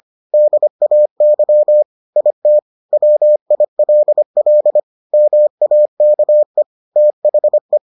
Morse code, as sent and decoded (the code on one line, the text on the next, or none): -.. .- -.-- / .. - / .-- .. .-.. .-.. / -- .- -.- . / - .... .
DAY IT WILL MAKE THE